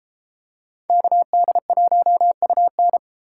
Morse code, heard as a keyed tone, about 32 wpm, 700 Hz. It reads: KD1UN